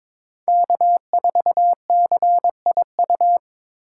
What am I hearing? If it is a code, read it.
Morse code, heard as a keyed tone, about 26 wpm, 700 Hz. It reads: K4CIU
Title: Speed Racing - USA 1x3 Call Signs 22wpm